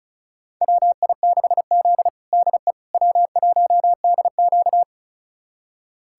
Morse code, heard as a keyed tone, about 34 wpm, 700 Hz. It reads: WI6Z DE W1DQ
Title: USA Callsign DE Callsign - 4 Characters - 35wpm